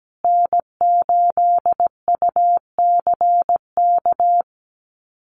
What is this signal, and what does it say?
Morse code, heard as a keyed tone, about 17 wpm, 700 Hz. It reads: N8UCK
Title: USA Call Signs Repeated 2x 17wpm